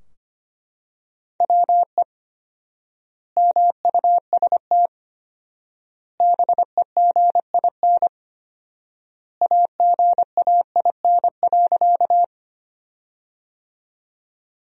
Morse code, heard as a keyed tone, about 25 wpm, 700 Hz. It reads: WE MUST BEGIN AGAIN.
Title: 4x Word-Spacing - Sentences from Top 300 Words - 25wpm